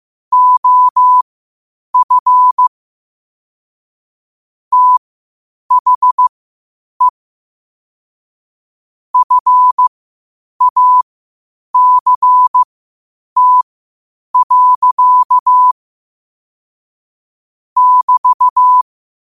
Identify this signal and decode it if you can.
Morse code, heard as a keyed tone, about 15 wpm, spaced as for 8 wpm, 1000 Hz. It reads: OF THE FACT. <BT>
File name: qotdfiles / 8wpm